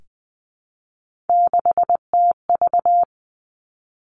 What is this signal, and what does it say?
Morse code, heard as a keyed tone, about 19 wpm, 700 Hz. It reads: 6TV
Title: Three Challenging Letter-Number Combinations 20wpm